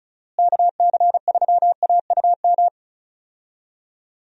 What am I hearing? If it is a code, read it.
Morse code, heard as a keyed tone, about 36 wpm, 700 Hz. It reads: KC3AUM
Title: USA Call Signs - 6 Characters - Repeated 2x - 35wpm